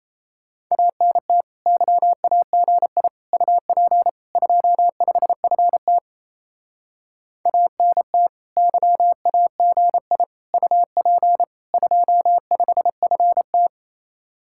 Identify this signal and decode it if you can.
Morse code, heard as a keyed tone, about 36 wpm, 700 Hz. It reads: ANT YAGI UP 25FT ANT YAGI UP 25FT